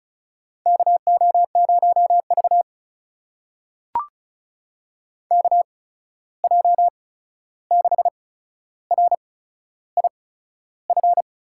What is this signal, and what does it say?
Morse code, heard as a keyed tone, about 36 wpm, spaced as for 13 wpm, 700 Hz. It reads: KO0V KJ6RIF
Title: USA Call Signs - 8x Character Spacing - 35wpm